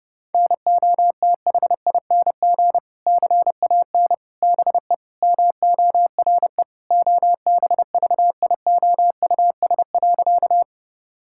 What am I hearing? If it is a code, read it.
Morse code, heard as a keyed tone, about 32 wpm, 700 Hz. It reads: NOTHING CAN BE MORE OBVIOUS.
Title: Sentences from Top 1600 Words 30wpm